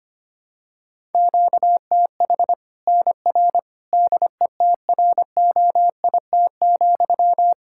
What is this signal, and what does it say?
Morse code, heard as a keyed tone, about 25 wpm, 700 Hz. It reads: QTH NR DETROIT,
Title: Practice QSO - Exchange 1 - Beginning - 25wpm